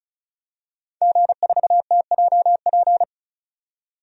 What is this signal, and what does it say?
Morse code, heard as a keyed tone, about 36 wpm, 700 Hz. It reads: G4TJP
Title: DXCC-100 Call Signs - 5 Characters - 35wpm